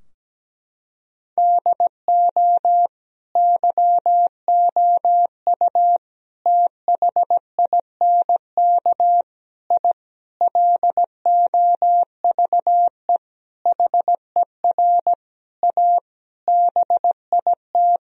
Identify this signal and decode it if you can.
Morse code, heard as a keyed tone, about 17 wpm, 700 Hz. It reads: DO YOU THINK I LOVE HER A BIT
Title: Sentences from Top 1000 Words - Review Part 2 - 17wpm